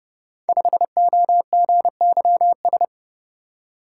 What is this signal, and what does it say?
Morse code, heard as a keyed tone, about 30 wpm, 700 Hz. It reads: 5OGYS
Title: Code Groups - 5 Letter:Numbers - 30wpm